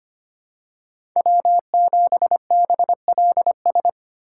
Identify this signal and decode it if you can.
Morse code, heard as a keyed tone, about 25 wpm, 700 Hz. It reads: W7BLS
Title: USA Call Signs - 5 Characters - 4x Character Spacing - 25wpm